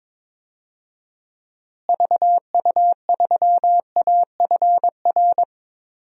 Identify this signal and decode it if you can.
Morse code, heard as a keyed tone, about 21 wpm, 700 Hz. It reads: VU3AFR
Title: DXCC-100 Call Signs - 6 Characters - Repeated 3x - 22wpm